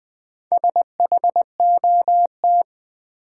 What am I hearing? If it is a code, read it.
Morse code, heard as a keyed tone, about 20 wpm, 700 Hz. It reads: SHOT
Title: Sentences from Top 1100 Words - New Word Drill Alt - 20wpm